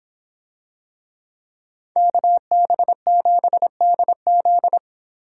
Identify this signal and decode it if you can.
Morse code, heard as a keyed tone, about 26 wpm, 700 Hz. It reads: KB7DZ